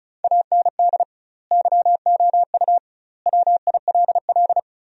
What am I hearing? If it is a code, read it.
Morse code, heard as a keyed tone, about 36 wpm, 700 Hz. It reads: AND YOU WILL
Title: 2x Word-Spacing - Sentences from Top 400 Words - 35wpm